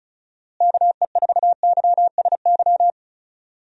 Morse code, heard as a keyed tone, about 36 wpm, 700 Hz. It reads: KE4YSY
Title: USA Call Signs - 6 Characters - Repeated 2x - 35wpm